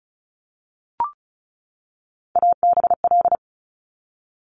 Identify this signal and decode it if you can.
Morse code, heard as a keyed tone, about 36 wpm, 700 Hz. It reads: ABL
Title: Top Three Letter Combinations in Words 35wpm